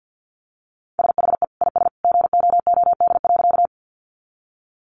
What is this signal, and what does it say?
Morse code, heard as a keyed tone, about 50 wpm, 700 Hz. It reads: SHE IS GOOD.
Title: Easy Phrases 50wpm